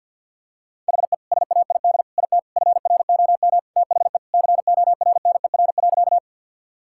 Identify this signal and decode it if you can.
Morse code, heard as a keyed tone, about 52 wpm, 700 Hz. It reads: HE SAID IT FROM THE CORNER.
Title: Sentences from Top 1600 Words 50wpm